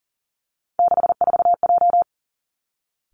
Morse code, heard as a keyed tone, about 42 wpm, 700 Hz. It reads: 64J
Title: Three Challenging Letter-Number Combinations 40wpm